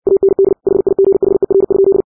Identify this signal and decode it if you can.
Morse code, heard as a keyed tone, about 60 wpm, 390 Hz. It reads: AND HIGHER?
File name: 60wpm